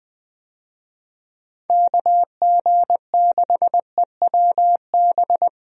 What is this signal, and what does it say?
Morse code, heard as a keyed tone, about 21 wpm, 700 Hz. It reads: KG6EWB
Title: USA Call Signs - 6 Characters - Repeated 3x - 20wpm